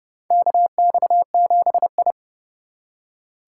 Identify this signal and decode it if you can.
Morse code, heard as a keyed tone, about 29 wpm, 700 Hz. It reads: KX7I
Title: USA Call Signs - 4 Characters - 6x Character Spacing - 30wpm